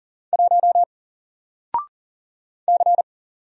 Morse code, heard as a keyed tone, about 39 wpm, 700 Hz. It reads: 1 C
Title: Single Letter-Number 40wpm